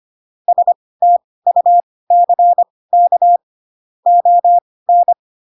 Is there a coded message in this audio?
Yes: Morse code from 0.5 to 5.1 s, heard as a keyed tone, about 25 wpm, spaced as for 18 wpm, 700 Hz. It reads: STUCK ON